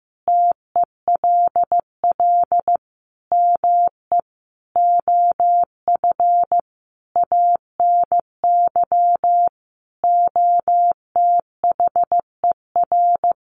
Morse code, heard as a keyed tone, about 15 wpm, 700 Hz. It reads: TELL ME OF ANY OTHER
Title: Sentences from Top 1400 Words 15wpm